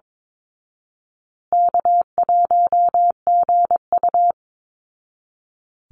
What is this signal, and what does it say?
Morse code, heard as a keyed tone, about 22 wpm, 700 Hz. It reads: K1GU
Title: Call Signs Repeated 2x - 22wpm